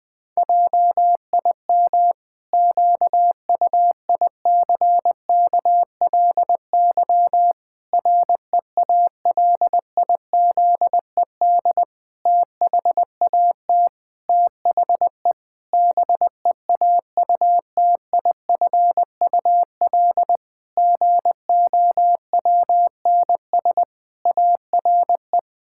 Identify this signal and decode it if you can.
Morse code, heard as a keyed tone, about 21 wpm, 700 Hz. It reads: JIM QUICKLY REALIZED THAT THE BEAUTIFUL GOWNS ARE
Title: Pangram Challenge v2 - 20wpm